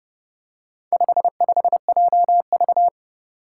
Morse code, heard as a keyed tone, about 29 wpm, 700 Hz. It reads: 55JV
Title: Four Challenging Letter-Number Combinations 30wpm